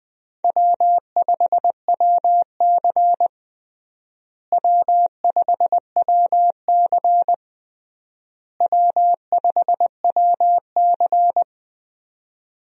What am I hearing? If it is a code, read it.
Morse code, heard as a keyed tone, about 21 wpm, 700 Hz. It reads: W5WC W5WC W5WC
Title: USA Call Signs - 4 Characters - Repeated 3x - 20wpm